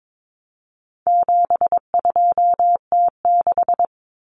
Morse code, heard as a keyed tone, about 22 wpm, 700 Hz. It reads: 72T6
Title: Common Cut Numbers - 4-Digit 22wpm